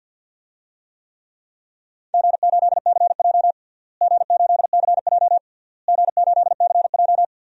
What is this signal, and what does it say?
Morse code, heard as a keyed tone, about 50 wpm, 700 Hz. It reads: G8CJ G8CJ G8CJ
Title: DXCC-100 Call Signs - 4 Characters - Repeated 3x - 50wpm